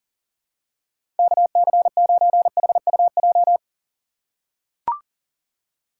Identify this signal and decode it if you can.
Morse code, heard as a keyed tone, about 40 wpm, 700 Hz. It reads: KC9HUJ
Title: USA Call Signs - 6 Characters - 3x Character Spacing - 40wpm